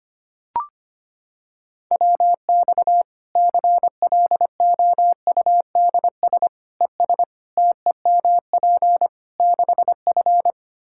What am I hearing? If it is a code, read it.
Morse code, heard as a keyed tone, about 25 wpm, 700 Hz. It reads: WX CLOUDS ES TEMP 6F
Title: Practice QSO - Exchange 2 - Info - 25wpm